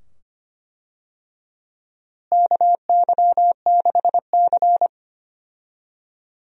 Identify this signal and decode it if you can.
Morse code, heard as a keyed tone, about 25 wpm, 700 Hz. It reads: KY6C
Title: USA Call Signs - 4 Characters - Repeated 2x - 25wpm